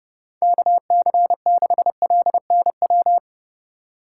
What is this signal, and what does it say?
Morse code, heard as a keyed tone, about 30 wpm, 700 Hz. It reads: KC6LNW
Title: USA Call Signs - 6 Characters - 2x Character Spacing - 30wpm